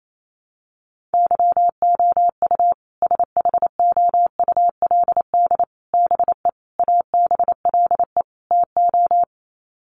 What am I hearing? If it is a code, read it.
Morse code, heard as a keyed tone, about 28 wpm, 700 Hz. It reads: YOU SHOULD BE ABLE TO
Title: Sets of 5 Words Encore 28wpm